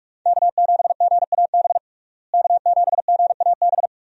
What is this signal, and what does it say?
Morse code, heard as a keyed tone, about 46 wpm, 700 Hz. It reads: K7GAB K7GAB
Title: USA Call Signs - 5 Characters - Repeated 2x - 45wpm